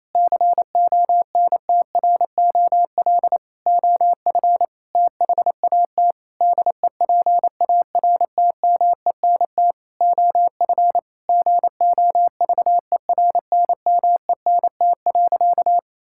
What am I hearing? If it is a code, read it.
Morse code, heard as a keyed tone, about 28 wpm, 700 Hz. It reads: CONTROL OF THAT DEPARTMENT OF GOVERNMENT.
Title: Sentences from Top 1000 Words - Review Part 5 - 28wpm